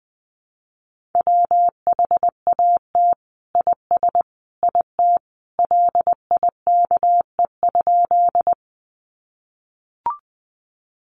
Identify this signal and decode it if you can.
Morse code, heard as a keyed tone, about 20 wpm, 700 Hz. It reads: WHAT IS IT LIKE?
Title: Sentences from Top 100 Words 20wpm